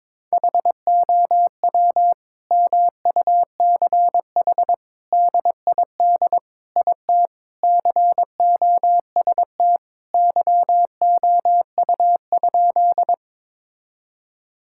Morse code, heard as a keyed tone, about 22 wpm, 700 Hz. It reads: HOW MUCH DID IT COST YOU?
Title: Sentences from Top 1000 Words - Review Part 5 - 22wpm